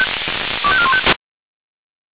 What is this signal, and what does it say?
Five-tone selective call (ZVEI), heard as quick runs of five tones, tones of about 70 ms. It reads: 25415
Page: SDR/ZVEI